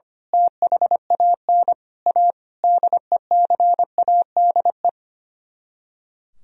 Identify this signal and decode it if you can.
Morse code, heard as a keyed tone, about 24 wpm, 700 Hz. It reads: THAN A DECADE
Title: Speed-Racing - Sets of 5 Words 25wpm